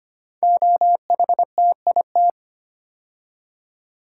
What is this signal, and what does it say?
Morse code, heard as a keyed tone, about 25 wpm, 700 Hz. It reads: OHTIT
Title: Code Groups - 5 Letters - 25wpm